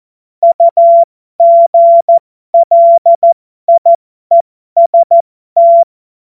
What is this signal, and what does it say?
Morse code, heard as a keyed tone, about 14 wpm, spaced as for 12 wpm, 680 Hz. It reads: UGLIEST
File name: Morsekurs / Witze